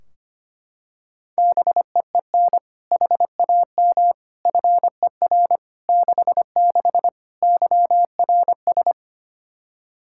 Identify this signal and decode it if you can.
Morse code, heard as a keyed tone, about 25 wpm, 700 Hz. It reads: BEEN HAM FER 66 YRS
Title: Practice QSO - Exchange 3 - More Info - 25wpm